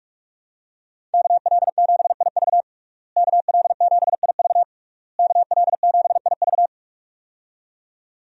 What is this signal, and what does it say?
Morse code, heard as a keyed tone, about 46 wpm, 700 Hz. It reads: KL7IV KL7IV KL7IV
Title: USA Call Signs - 5 Characters - Repeated 3x - 45wpm